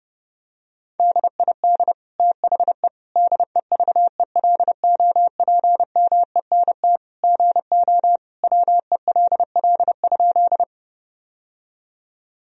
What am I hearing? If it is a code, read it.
Morse code, heard as a keyed tone, about 32 wpm, 700 Hz. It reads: DID THE DEVELOPMENT GO WELL?
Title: Sentences from Top 1000 Words - Review Part 4 - 30wpm